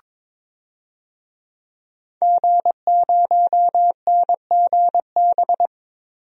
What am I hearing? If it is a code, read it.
Morse code, heard as a keyed tone, about 22 wpm, 700 Hz. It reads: G0NGB
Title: DXCC-100 Call Signs - 5 Characters - Repeated 3x - 22wpm